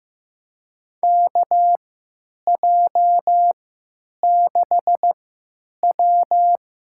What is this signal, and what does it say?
Morse code, heard as a keyed tone, about 15 wpm, spaced as for 9 wpm, 700 Hz. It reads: KJ6W
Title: USA Call Signs - 3x Character Spacing - 15wpm